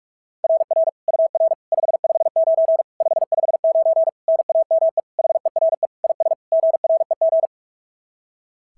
Morse code, heard as a keyed tone, about 46 wpm, 690 Hz. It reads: RR UR 559 559 NAME HERE IS GREG
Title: SOTA QSO Practice 45wpm